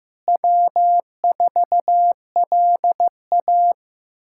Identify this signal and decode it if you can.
Morse code, heard as a keyed tone, about 15 wpm, 700 Hz. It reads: W4LA
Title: USA Callsign DE Callsign - 4 Characters - 15wpm